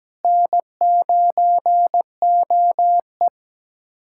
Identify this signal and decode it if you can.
Morse code, heard as a keyed tone, about 17 wpm, 700 Hz. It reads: N9OE
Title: USA Callsign DE Callsign - 4 Characters - 17wpm